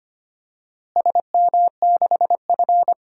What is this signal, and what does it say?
Morse code, heard as a keyed tone, about 25 wpm, 700 Hz. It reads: SM6F
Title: DXCC-100 Call Signs - 4 Characters - Repeated 3x - 25wpm